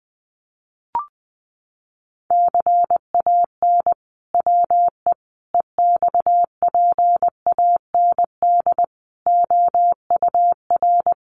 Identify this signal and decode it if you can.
Morse code, heard as a keyed tone, about 20 wpm, 700 Hz. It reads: CAN WE EXPAND OUR
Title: Sentences from Top 1400 Words 20wpm